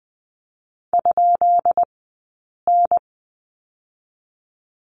Morse code, heard as a keyed tone, about 20 wpm, spaced as for 8 wpm, 700 Hz. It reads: ?N
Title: 165 Lesson 27d - Number 7 - Two Characters 20wpm